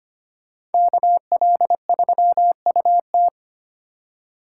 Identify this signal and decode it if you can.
Morse code, heard as a keyed tone, about 29 wpm, 700 Hz. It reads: KL3UT
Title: Speed Racing - USA Call Signs 25wpm